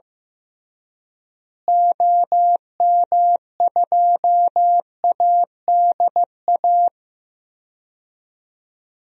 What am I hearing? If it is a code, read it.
Morse code, heard as a keyed tone, about 15 wpm, 700 Hz. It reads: OM2ADA
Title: DXCC-100 Call Signs - 6 Characters - Repeated 3x - 15wpm